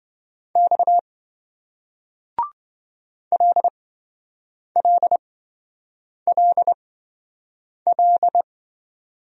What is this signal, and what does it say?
Morse code, heard as a keyed tone, about 27 wpm, spaced as for 7 wpm, 700 Hz. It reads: X LLLL